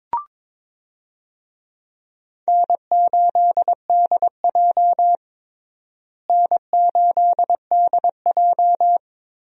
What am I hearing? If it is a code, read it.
Morse code, heard as a keyed tone, about 22 wpm, 700 Hz. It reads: N8DJ N8DJ
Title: USA Call Signs - 4 Characters - Repeated 2x - 22wpm